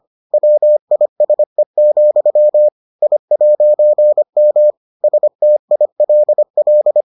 Morse code, heard as a keyed tone, about 25 wpm, 600 Hz. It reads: WISE, I'M STILL